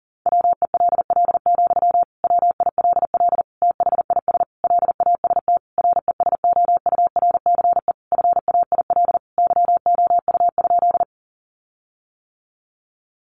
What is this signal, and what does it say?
Morse code, heard as a keyed tone, about 40 wpm, 700 Hz. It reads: WELL, WILL THIS LAST RESOURCE FAIL YOU?
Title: Sentences from Top 1300 Words 40wpm